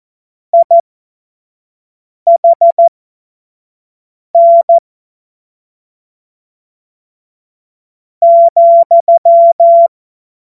Morse code, heard as a keyed tone, about 14 wpm, spaced as for 4 wpm, 680 Hz. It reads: IHN,
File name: Morsekurs / Witze / Witz 07. Februar 2023 25 F70